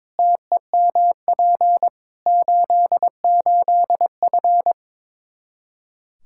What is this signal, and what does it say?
Morse code, heard as a keyed tone, about 22 wpm, 700 Hz. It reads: TEMP 88F
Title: Practice QSO - Exchange 2 - Info - 22wpm